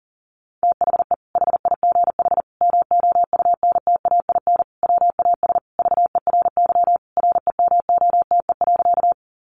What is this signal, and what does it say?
Morse code, heard as a keyed tone, about 40 wpm, 700 Hz. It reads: THE HIGH MOUNTAIN WAS VERY REMOTE.
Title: Sentences from Top 1300 Words 40wpm